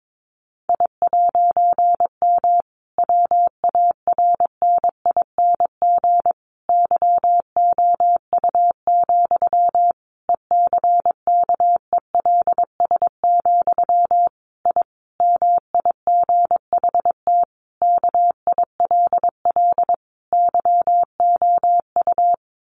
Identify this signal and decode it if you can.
Morse code, heard as a keyed tone, about 22 wpm, 700 Hz. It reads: I'M WARNING YOU, ECKELS, I MIGHT KILL YOU
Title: A Sound Of Thunder 22wpm